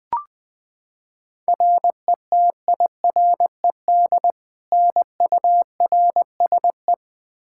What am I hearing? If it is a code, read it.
Morse code, heard as a keyed tone, about 20 wpm, 700 Hz. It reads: RETIRED NURSE
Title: Practice QSO - Exchange 3 - More Info - 20wpm